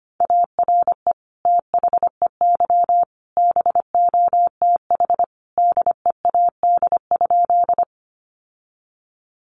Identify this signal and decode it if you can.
Morse code, heard as a keyed tone, about 25 wpm, 700 Hz. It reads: ARE THEY BOTH DEAD?
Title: Sentences from Top 1500 Words 25wpm